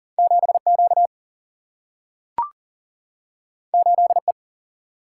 Morse code, heard as a keyed tone, about 40 wpm, 700 Hz. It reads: 7 Q 8 E
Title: Single-Letter Single-Number 40wpm